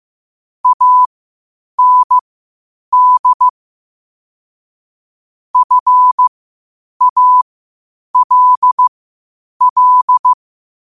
Morse code, heard as a keyed tone, about 15 wpm, spaced as for 8 wpm, 1000 Hz. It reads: AND FALL